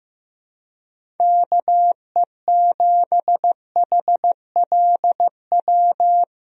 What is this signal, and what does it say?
Morse code, heard as a keyed tone, about 15 wpm, 700 Hz. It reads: KE7HLW
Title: USA Call Signs - 6 Characters - Repeated 2x - 15wpm